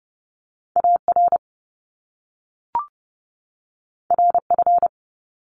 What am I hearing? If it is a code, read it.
Morse code, heard as a keyed tone, about 30 wpm, 700 Hz. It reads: AR RF
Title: Top Two Letter Combinations in Words 30wpm